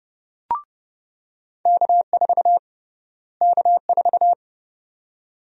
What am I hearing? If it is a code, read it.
Morse code, heard as a keyed tone, about 30 wpm, 700 Hz. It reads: K4 K4
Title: USA Call Sign Prefixes - Repeated 2x - 30wpm